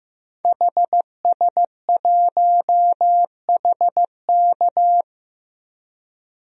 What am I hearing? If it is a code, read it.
Morse code, heard as a keyed tone, about 15 wpm, 700 Hz. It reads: HS1HK
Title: DXCC-100 Call Signs - 5 Characters - 15wpm